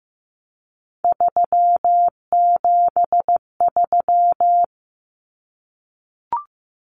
Morse code, heard as a keyed tone, about 15 wpm, 700 Hz. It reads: 373
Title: Numbers - 3-Digit 15wpm